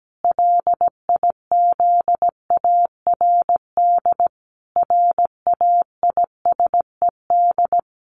Morse code, heard as a keyed tone, about 17 wpm, 700 Hz. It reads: LIZARD RAISED
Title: A Sound Of Thunder 17wpm